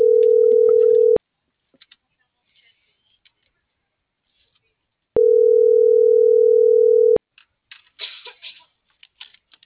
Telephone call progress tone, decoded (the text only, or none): ringback tone